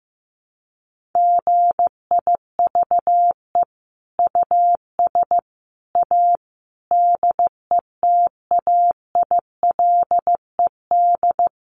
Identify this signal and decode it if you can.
Morse code, heard as a keyed tone, about 15 wpm, 700 Hz. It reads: GIVE US A DETAILED
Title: Sentences from Top 1500 Words 15wpm